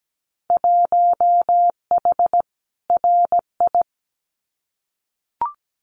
Morse code, heard as a keyed tone, about 17 wpm, 700 Hz. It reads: 1H RI
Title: Winter Field Day Exchanges 17wpm